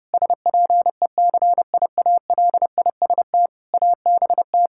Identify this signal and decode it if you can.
Morse code, heard as a keyed tone, about 30 wpm, 700 Hz. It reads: SPECIALIST ABT
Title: Practice QSO - Exchange 3 - More Info - 30wpm